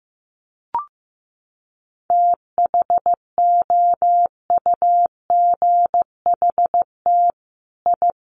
Morse code, heard as a keyed tone, about 15 wpm, 700 Hz. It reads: THOUGHT I
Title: Sets of 5 Words Encore2 15wpm